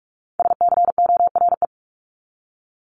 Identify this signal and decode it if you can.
Morse code, heard as a keyed tone, about 44 wpm, 700 Hz. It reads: SCORE
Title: Top 1501-1750 Words 45wpm